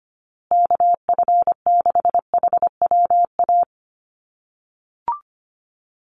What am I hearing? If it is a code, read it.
Morse code, heard as a keyed tone, about 25 wpm, 700 Hz. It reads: KF6HWA
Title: USA Call Signs Repeated 2x 25wpm